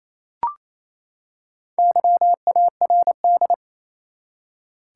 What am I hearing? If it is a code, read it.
Morse code, heard as a keyed tone, about 28 wpm, 700 Hz. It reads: YARD
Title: Four Letter Words - 28wpm